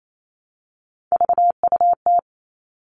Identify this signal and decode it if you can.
Morse code, heard as a keyed tone, about 28 wpm, 700 Hz. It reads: VUT
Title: All Cut Numbers - 3-Digit 28wpm